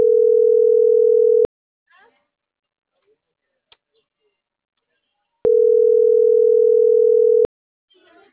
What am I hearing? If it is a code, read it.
Ringback tone (North America), 440 + 480 Hz.